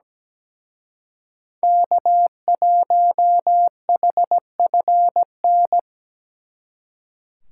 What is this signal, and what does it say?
Morse code, heard as a keyed tone, about 20 wpm, 700 Hz. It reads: K1HFN
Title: Speed Racing - USA 1x3 Call Signs 17wpm